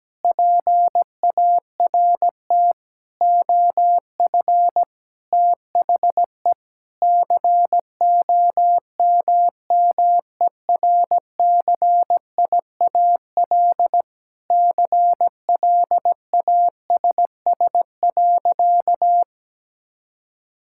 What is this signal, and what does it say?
Morse code, heard as a keyed tone, about 17 wpm, 700 Hz. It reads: PART OF THE COMMERCIAL CLASS.
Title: Sentences from Top 1000 Words - Review Part 5 - 17wpm